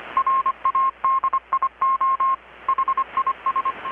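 Morse code, heard as a keyed tone, about 25 wpm, 1070 Hz. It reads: RADIO HIS
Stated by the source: Morse from KPH's Night of Nights XX maritime radiotelegraphy event, received in Chiapas, Mexico about 3,800 km away on a Kenwood R-600 with a grounded 10 m long-wire antenna.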